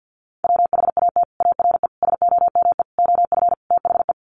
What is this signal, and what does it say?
Morse code, heard as a keyed tone, about 50 wpm, 700 Hz. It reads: WHAT ARE SOME OF THE
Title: Sets of 5 Words Encore 50wpm